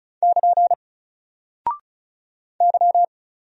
Morse code, heard as a keyed tone, about 35 wpm, 700 Hz. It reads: <KN> Y
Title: Single Character (Letter:Number:Punctuation:Prosign) 35wpm